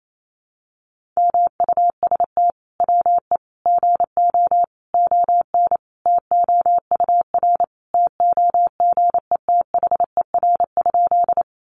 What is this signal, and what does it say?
Morse code, heard as a keyed tone, about 28 wpm, 700 Hz. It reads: MUST WE GO ON TOUR TOGETHER?
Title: Sentences from Top 1400 Words 28wpm